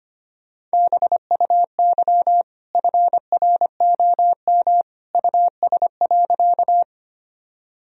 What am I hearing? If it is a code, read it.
Morse code, heard as a keyed tone, about 25 wpm, 700 Hz. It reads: BUY FROM US.
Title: Sentences from Top 1000 Words - Review Part 4 - 25wpm